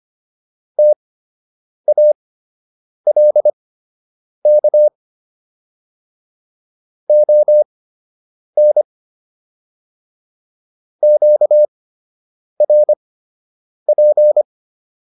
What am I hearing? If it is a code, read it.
Morse code, heard as a keyed tone, about 25 wpm, spaced as for 8 wpm, 600 Hz. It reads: TALK ON QRP